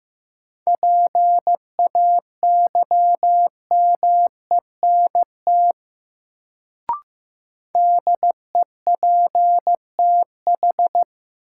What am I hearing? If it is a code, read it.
Morse code, heard as a keyed tone, about 15 wpm, 700 Hz. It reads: PAYMENT DEPTH
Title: Sentences from Top 1200 Words - New Word Drill - 15wpm